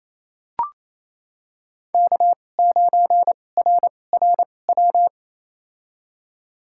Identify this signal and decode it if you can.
Morse code, heard as a keyed tone, about 28 wpm, spaced as for 20 wpm, 700 Hz. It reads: K9RRW
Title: USA Call Signs - 2x Character Spacing - 28wpm